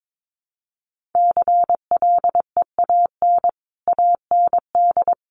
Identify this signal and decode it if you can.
Morse code, heard as a keyed tone, about 22 wpm, 700 Hz. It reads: CLEAN AND
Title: Sentences from Top 1100 Words 22wpm